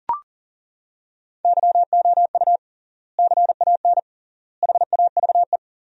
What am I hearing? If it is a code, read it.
Morse code, heard as a keyed tone, about 40 wpm, 700 Hz. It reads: YOU CAN HAVE
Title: 3x Word-Spacing - Sentences from Top 100 Words - 40wpm